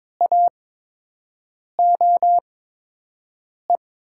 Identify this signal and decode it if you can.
Morse code, heard as a keyed tone, about 22 wpm, spaced as for 6 wpm, 700 Hz. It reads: AOE
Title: USA Call Signs - 6 Characters - 8x Character Spacing - 22wpm